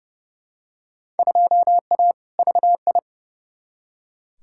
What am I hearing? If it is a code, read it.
Morse code, heard as a keyed tone, about 30 wpm, 700 Hz. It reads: 2A VI
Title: Field Day Exchanges - 30wpm